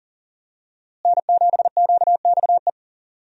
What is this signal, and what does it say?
Morse code, heard as a keyed tone, about 40 wpm, 700 Hz. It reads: N7QXE
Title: USA Call Signs - 5 Characters - Repeated 2x - 40wpm